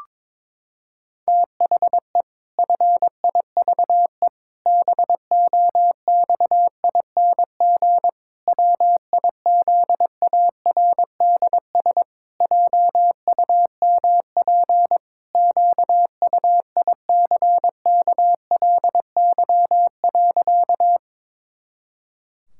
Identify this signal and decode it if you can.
Morse code, heard as a keyed tone, about 22 wpm, 700 Hz. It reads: THE FIVE BOXING WIZARDS JUMP QUICKLY.
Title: Pangram Challenge v2 - 22wpm